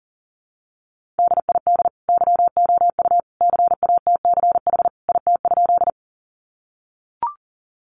Morse code, heard as a keyed tone, about 40 wpm, 700 Hz. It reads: DID YOU CATCH IT?
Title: Sentences from Top 1500 Words 40wpm